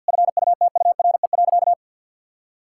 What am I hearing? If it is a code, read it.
Morse code, heard as a keyed tone, about 50 wpm, 700 Hz. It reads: FUTURE.
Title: Sentences from Top 1000 Words - Review Part 6 - 50wpm